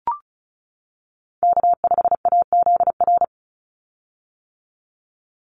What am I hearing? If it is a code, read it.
Morse code, heard as a keyed tone, about 34 wpm, 700 Hz. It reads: K5AZR
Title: Call Signs - 35wpm